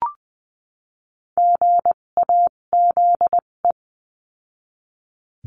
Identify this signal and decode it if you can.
Morse code, heard as a keyed tone, about 20 wpm, spaced as for 17 wpm, 700 Hz. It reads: GAZE
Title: 257 Lesson 42b - Words 14-20wpm